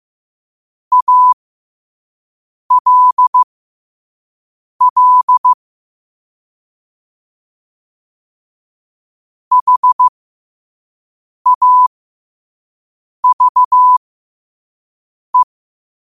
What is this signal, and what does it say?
Morse code, heard as a keyed tone, about 15 wpm, spaced as for 5 wpm, 1000 Hz. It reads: ALL HAVE
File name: qotdfiles / 5wpm